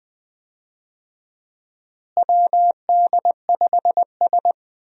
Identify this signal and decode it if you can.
Morse code, heard as a keyed tone, about 20 wpm, 700 Hz. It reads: WD5S
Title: USA Call Signs - 4 Characters - Repeated 2x - 20wpm